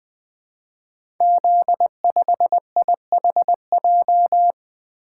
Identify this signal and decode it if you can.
Morse code, heard as a keyed tone, about 20 wpm, 700 Hz. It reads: Z5IHJ
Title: Code Groups - 5 Letter:Numbers - 20wpm